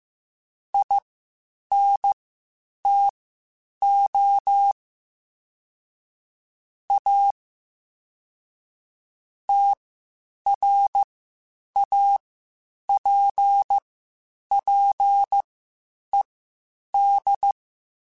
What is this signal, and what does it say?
Morse code, heard as a keyed tone, about 15 wpm, spaced as for 8 wpm, 800 Hz. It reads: INTO A TRAPPED
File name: QSOs Spaced-mp3s/kc7haa @13 sp